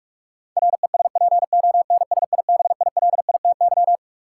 Morse code, heard as a keyed tone, about 44 wpm, 700 Hz. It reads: RESPONSIBILITY
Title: Sentences from Top 1200 Words - New Word Drill - 45wpm